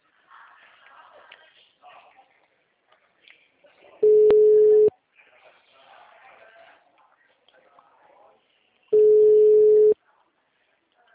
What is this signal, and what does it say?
Ringback tone (Europe, 425 Hz), with speech.
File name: out- -2019- - - .66440